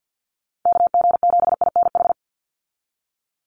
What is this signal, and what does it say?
Morse code, heard as a keyed tone, about 50 wpm, 700 Hz. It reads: KG7INH
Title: USA Call Signs 50wpm